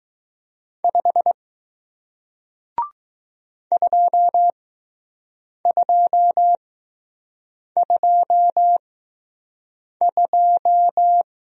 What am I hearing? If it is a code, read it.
Morse code, heard as a keyed tone, about 20 wpm, spaced as for 6 wpm, 700 Hz. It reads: 5 2222